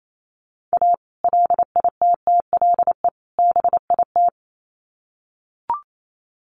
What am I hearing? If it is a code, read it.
Morse code, heard as a keyed tone, about 28 wpm, 700 Hz. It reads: A LITTLE BIT
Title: Sets of 3 Words 28wpm